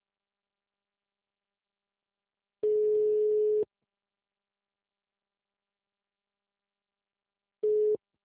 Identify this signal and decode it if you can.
Ringback tone (Europe, 425 Hz).
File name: out- -2002- - - .48032